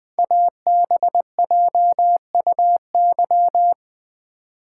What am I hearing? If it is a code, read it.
Morse code, heard as a keyed tone, about 20 wpm, 700 Hz. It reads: ABJUY
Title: Code Groups - 5 Letters - 20wpm